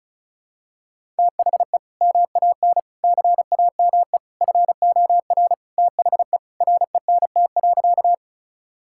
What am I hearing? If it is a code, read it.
Morse code, heard as a keyed tone, about 34 wpm, 700 Hz. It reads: THE MAN CAME FOR THE RENT.
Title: Sentences from Top 1000 Words - Review Part 5 - 35wpm